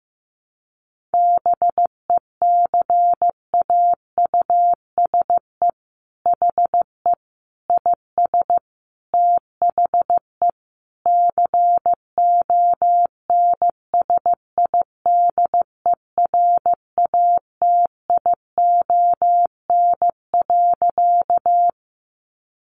Morse code, heard as a keyed tone, about 15 wpm, 700 Hz. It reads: BECAUSE HE IS THE CONSIDERATION.
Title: Sentences from Top 1500 Words 15wpm